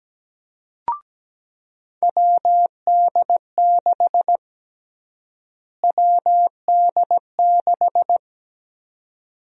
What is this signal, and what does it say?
Morse code, heard as a keyed tone, about 17 wpm, 700 Hz. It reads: WD6 WD6
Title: USA Call Sign Prefixes - Repeated 2x - 17wpm